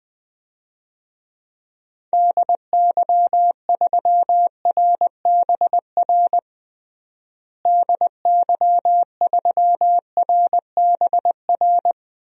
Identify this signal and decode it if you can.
Morse code, heard as a keyed tone, about 20 wpm, 700 Hz. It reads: DY3RBR DY3RBR
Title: DXCC-100 Call Signs - 6 Characters - Repeated 2x - 20wpm